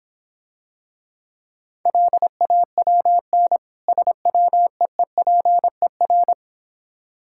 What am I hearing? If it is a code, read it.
Morse code, heard as a keyed tone, about 26 wpm, 690 Hz. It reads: LAWN SWEEPER